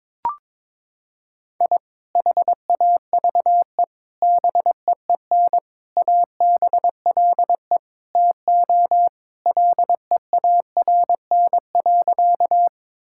Morse code, heard as a keyed tone, about 22 wpm, 700 Hz. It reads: I HAVE BEEN ABLE TO LEARN.
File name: practice / 22wpm / Sentences with 3-Word Phrases - 22wpm